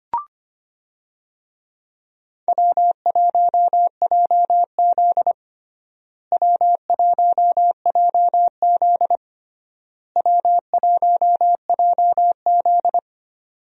Morse code, heard as a keyed tone, about 25 wpm, 700 Hz. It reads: W1JZ W1JZ W1JZ
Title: USA Call Signs - 4 Characters - Repeated 3x - 25wpm